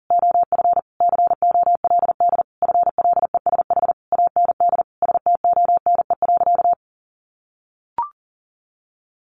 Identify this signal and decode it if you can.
Morse code, heard as a keyed tone, about 40 wpm, 700 Hz. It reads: OF COLD FLESH AND STONE.
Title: A Sound Of Thunder 40wpm